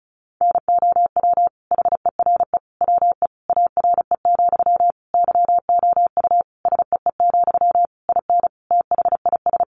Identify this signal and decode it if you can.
Morse code, heard as a keyed tone, about 34 wpm, 700 Hz. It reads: NOW HERE WE ARE, YOU SEE, IN THIS
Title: Sentences from Top 1200 Words 35wpm